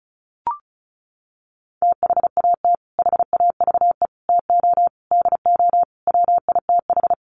Morse code, heard as a keyed tone, about 34 wpm, 700 Hz. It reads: THAT HAVE TO DO WITH
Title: Sets of 5 Words Encore2 35wpm